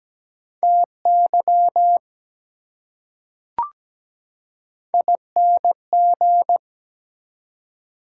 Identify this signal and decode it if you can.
Morse code, heard as a keyed tone, about 17 wpm, 700 Hz. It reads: TY ING
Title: Prefixes and Suffixes 17wpm